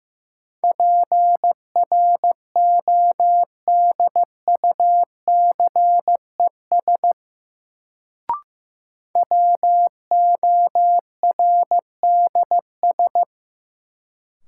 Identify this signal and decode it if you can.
Morse code, heard as a keyed tone, about 15 wpm, 700 Hz. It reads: PRODUCES WORDS
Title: Sentences from Top 1000 Words - Review Part 6 - New Word Drill - 15wpm